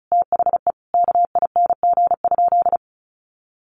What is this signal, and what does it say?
Morse code, heard as a keyed tone, about 34 wpm, 700 Hz. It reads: THE KING?
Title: Sentences from Top 1400 Words 35wpm